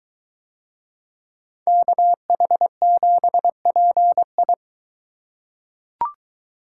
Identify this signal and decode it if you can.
Morse code, heard as a keyed tone, about 21 wpm, 700 Hz. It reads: KH7PI